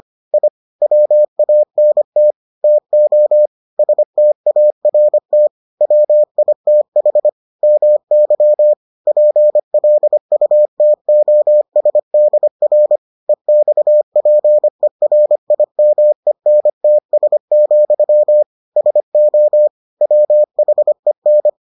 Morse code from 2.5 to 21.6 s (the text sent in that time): TO START WITH MY PLUTOSDR EXPERIMENTS, SO WHEN